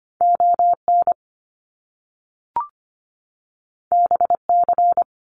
Morse code, heard as a keyed tone, about 25 wpm, 700 Hz. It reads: ON BC
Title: Canadian Provinces and Territories 25wpm